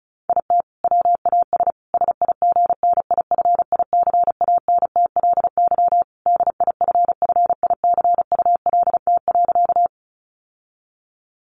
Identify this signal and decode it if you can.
Morse code, heard as a keyed tone, about 34 wpm, 700 Hz. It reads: IT WAS SIGNIFICANTLY DIFFICULT.
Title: Sentences from Top 1500 Words 35wpm